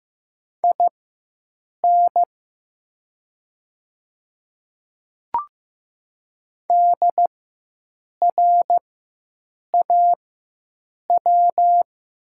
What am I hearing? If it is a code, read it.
Morse code, heard as a keyed tone, about 15 wpm, spaced as for 7 wpm, 700 Hz. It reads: IN DRAW
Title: 4x Character Spacing - Top 300 Words - 15wpm